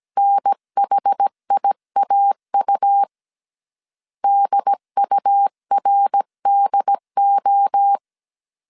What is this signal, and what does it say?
Morse code, heard as a keyed tone, about 17 wpm, 800 Hz. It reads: NHIAU DURDO
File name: CW / DET / 1 3 1500C S A 16wpm 2141